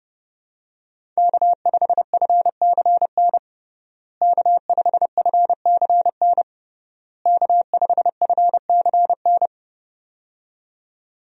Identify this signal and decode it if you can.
Morse code, heard as a keyed tone, about 32 wpm, 700 Hz. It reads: K5FCN K5FCN K5FCN
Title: USA Call Signs - 5 Characters - Repeated 3x - 30wpm